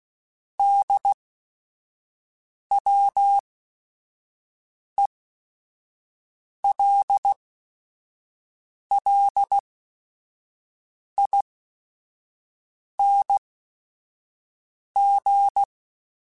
Morse code, heard as a keyed tone, about 16 wpm, spaced as for 5 wpm, 800 Hz. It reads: DWELLING